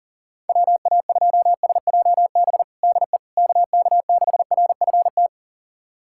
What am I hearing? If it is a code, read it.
Morse code, heard as a keyed tone, about 37 wpm, 700 Hz. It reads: WA2SJB DE KK6RFT
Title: USA Callsign DE Callsign - 6 Characters - 40wpm